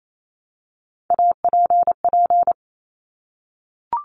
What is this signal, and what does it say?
Morse code, heard as a keyed tone, about 28 wpm, 700 Hz. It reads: APP
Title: Top Three Letter Combinations in Words 28wpm